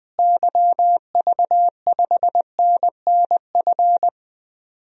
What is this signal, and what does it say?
Morse code, heard as a keyed tone, about 20 wpm, 700 Hz. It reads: YV5NNF
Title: DXCC-100 Call Signs - 6 Characters - Repeated 3x - 20wpm